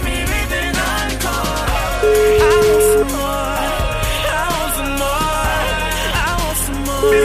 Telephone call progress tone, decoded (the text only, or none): ringback tone